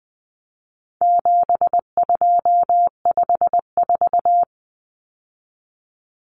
Morse code, heard as a keyed tone, about 20 wpm, 700 Hz. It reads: 7254
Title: Common Cut Numbers - 4-Digit 20wpm